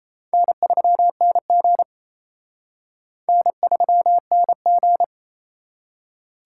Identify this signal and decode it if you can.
Morse code, heard as a keyed tone, about 38 wpm, 700 Hz. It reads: N3NG N3NG